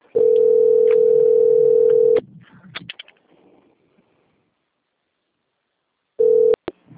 Ringback tone (North America), 440 + 480 Hz.